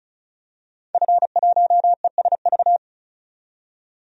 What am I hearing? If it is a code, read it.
Morse code, heard as a keyed tone, about 36 wpm, 700 Hz. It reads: F1ESV
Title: DXCC-100 Call Signs - 5 Characters - 35wpm